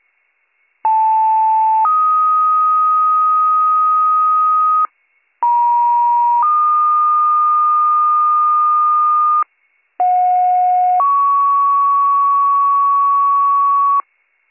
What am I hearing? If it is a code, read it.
Two-tone page, heard as a one-second tone, then a three-second tone, about 880 then 1250 Hz, about 950 then 1220 Hz, about 730 then 1120 Hz. It is heard from 0.8 to 14.0 s.